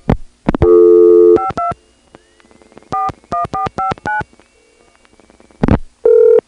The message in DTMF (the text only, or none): 52 41459